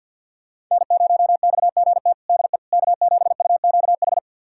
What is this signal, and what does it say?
Morse code, heard as a keyed tone, about 51 wpm, 700 Hz. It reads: N0XGT DE K7UQH
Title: USA Callsign DE Callsign - 5 Characters - 50wpm